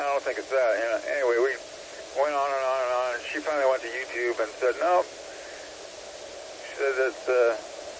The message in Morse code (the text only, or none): none